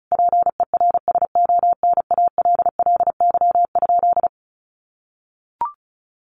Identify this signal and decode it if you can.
Morse code, heard as a keyed tone, about 34 wpm, 700 Hz. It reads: PERSONALLY?
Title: Sentences from Top 1400 Words 35wpm